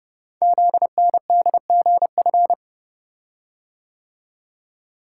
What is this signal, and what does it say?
Morse code, heard as a keyed tone, about 30 wpm, 700 Hz. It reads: ZNDGF
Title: Code Groups - 5 Letters - 30wpm